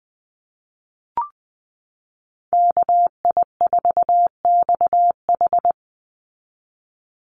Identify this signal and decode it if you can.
Morse code, heard as a keyed tone, about 20 wpm, 700 Hz. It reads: KI4XH
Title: Call Signs Repeated 3x - 20wpm